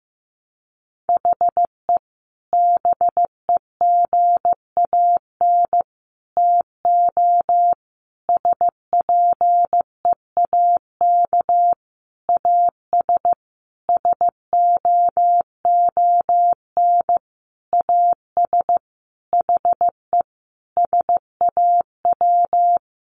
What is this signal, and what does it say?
Morse code, heard as a keyed tone, about 15 wpm, 700 Hz. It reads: HE BEGAN TO SPEAK AS SOON AS HE SAW
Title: Sentences from Top 1100 Words 15wpm